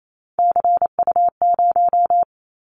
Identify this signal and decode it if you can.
Morse code, heard as a keyed tone, about 28 wpm, 700 Hz. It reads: CU0
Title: Callsign Prefixes for Top 50 DXCC - 28wpm